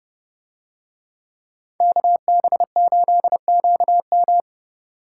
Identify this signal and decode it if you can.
Morse code, heard as a keyed tone, about 29 wpm, 700 Hz. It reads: KB8QM